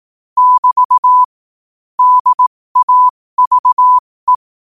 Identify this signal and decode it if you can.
Morse code, heard as a keyed tone, about 18 wpm, spaced as for 15 wpm, 1000 Hz. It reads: <BT> DAVE
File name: qotdfiles / 15wpm